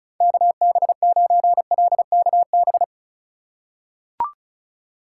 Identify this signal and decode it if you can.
Morse code, heard as a keyed tone, about 34 wpm, 700 Hz. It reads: KB9LKB
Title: USA Call Signs - 6 Characters - 4x Character Spacing - 35wpm